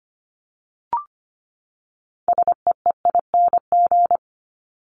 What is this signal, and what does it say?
Morse code, heard as a keyed tone, about 25 wpm, 700 Hz. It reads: SEEING
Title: ING Suffix - Top 100 Words with Base Words 25wpm